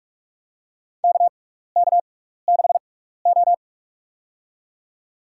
Morse code, heard as a keyed tone, about 45 wpm, spaced as for 15 wpm, 700 Hz. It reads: KK6O
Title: USA Call Signs - 6x Character Spacing - 45wpm